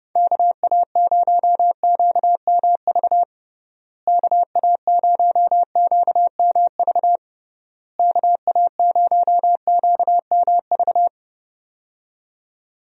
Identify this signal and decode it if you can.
Morse code, heard as a keyed tone, about 32 wpm, 700 Hz. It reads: KA0QMV KA0QMV KA0QMV
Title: USA Call Signs - 6 Characters - Repeated 3x - 30wpm